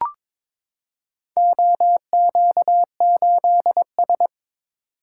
Morse code, heard as a keyed tone, about 21 wpm, 700 Hz. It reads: OQ8S
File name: practice / 22wpm / Four Letter-Number Combinations - 22wpm